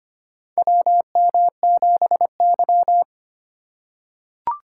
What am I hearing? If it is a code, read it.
Morse code, heard as a keyed tone, about 25 wpm, 700 Hz. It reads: WM7Y
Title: USA Call Signs - 4 Characters - 3x Character Spacing - 25wpm